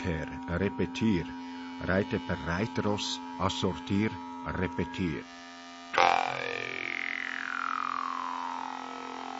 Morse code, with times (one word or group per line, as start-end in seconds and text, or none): none